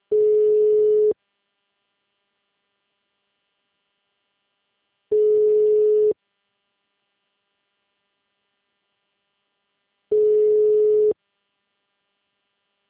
Ringback tone (Europe, 425 Hz).